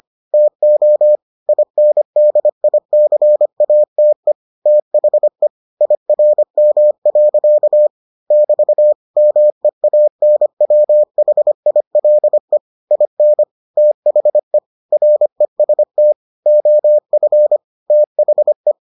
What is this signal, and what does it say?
Morse code, heard as a keyed tone, about 25 wpm, 600 Hz. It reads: TO INDICATE THE IRM. <BT> MEANWHILE IN THE REST OF THE